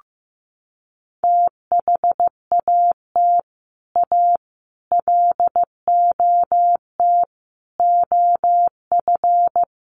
Morse code, heard as a keyed tone, about 15 wpm, 700 Hz. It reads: THAT A LOT OF
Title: Sets of 5 Words Encore2 15wpm